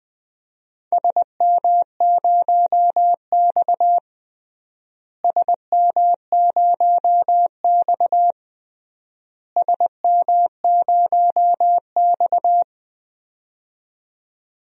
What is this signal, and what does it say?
Morse code, heard as a keyed tone, about 20 wpm, 700 Hz. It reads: SM0X SM0X SM0X
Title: DXCC-100 Call Signs - 4 Characters - Repeated 3x - 20wpm